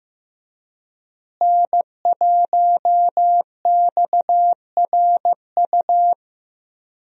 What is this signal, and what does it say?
Morse code, heard as a keyed tone, about 15 wpm, 700 Hz. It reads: N1XRU
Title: USA Call Signs - 5 Characters - Repeated 2x - 15wpm